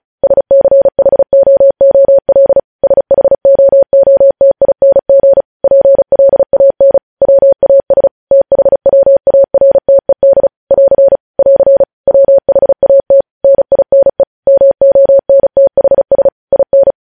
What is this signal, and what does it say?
Morse code, heard as a keyed tone, about 36 wpm, 550 Hz. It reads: SCHOOL SHOOTING PLAN WAS THWARTED <AR> <AR> WHAT NINE MONTHS IN
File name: podcast / News headlines-1x-35x35